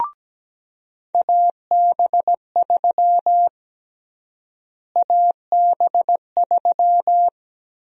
Morse code, heard as a keyed tone, about 17 wpm, 700 Hz. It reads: AB3 AB3
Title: USA Call Sign Prefixes - Repeated 2x - 17wpm